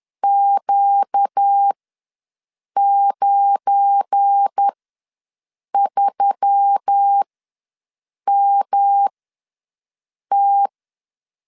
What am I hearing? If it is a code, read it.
Morse code, heard as a keyed tone, about 10 wpm, spaced as for 6 wpm, 800 Hz. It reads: Q93MT
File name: LEZIONI / 8 YJ45 / 8 YJ ws 10wpm 4043